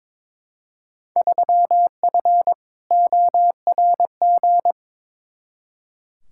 Morse code, heard as a keyed tone, about 22 wpm, 700 Hz. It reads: 3F ORG
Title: Field Day Exchanges - 22wpm